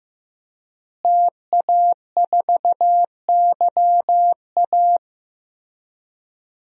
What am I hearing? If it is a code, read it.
Morse code, heard as a keyed tone, about 16 wpm, 700 Hz. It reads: TA4YA
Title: DXCC-100 Call Signs - 5 Characters - Repeated 3x - 15wpm